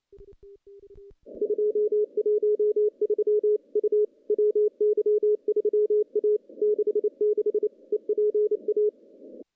Morse code, heard as a keyed tone, about 28 wpm, 410 Hz. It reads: 213 U WY3A 66 EPA
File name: contests / CW-SS-k5tr- -0100